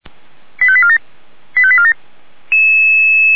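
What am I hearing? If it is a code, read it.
Five-tone selective call (ZVEI), heard as quick runs of five tones, tones of about 70 ms. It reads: 85847 85847
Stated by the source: Five-tone ZVEI selective call 89354, received by SDR.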